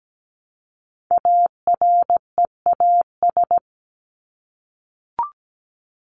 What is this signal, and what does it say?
Morse code, heard as a keyed tone, about 17 wpm, 700 Hz. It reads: AREAS
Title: ES:S Suffix - Top 100 Words with Base Words 17wpm